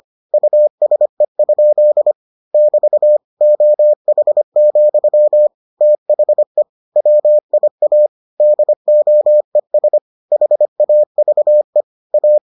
Morse code, heard as a keyed tone, about 25 wpm, 600 Hz. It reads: USE? <BT> OH, THE WIA DOES HAVE A